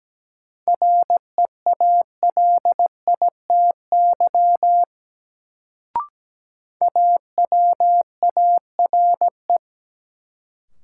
Morse code, heard as a keyed tone, about 17 wpm, 700 Hz. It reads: REALITY AWARE
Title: Sentences from Top 1200 Words - New Word Drill Alt - 17wpm